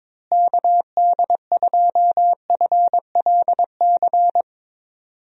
Morse code, heard as a keyed tone, about 22 wpm, 700 Hz. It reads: KD2FLC
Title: USA Call Signs - 6 Characters - Repeated 2x - 22wpm